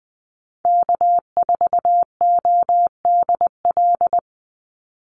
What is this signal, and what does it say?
Morse code, heard as a keyed tone, about 20 wpm, 700 Hz. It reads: K4ODL
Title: Call Signs - 20wpm